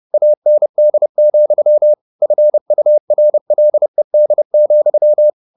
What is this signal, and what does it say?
Morse code, heard as a keyed tone, about 30 wpm, 600 Hz. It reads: AND, FURLED,
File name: the old man and the sea / the old man and the sea 1 30wpm